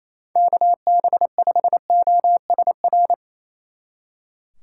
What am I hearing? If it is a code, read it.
Morse code, heard as a keyed tone, about 34 wpm, 700 Hz. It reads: KB5OSR
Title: Speed Racing - USA 2x3 Call Signs 28wpm